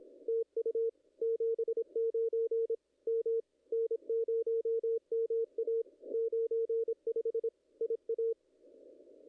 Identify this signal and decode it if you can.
Morse code, heard as a keyed tone, about 22 wpm, 440 Hz. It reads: TU 79 M N0MA 95 IA
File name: contests / ss-cw-k5tr-op-n5rz -0000